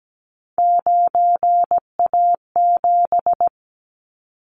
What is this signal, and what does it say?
Morse code, heard as a keyed tone, about 17 wpm, 700 Hz. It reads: 9A7
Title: Callsign Prefixes for Top 50 DXCC - 17wpm